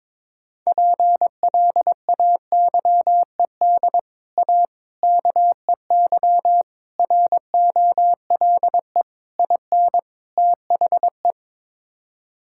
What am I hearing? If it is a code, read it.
Morse code, heard as a keyed tone, about 22 wpm, 700 Hz. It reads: PLAYED A KEY ROLE IN THE
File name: practice / 22wpm / Sets of 6 Words - 22wpm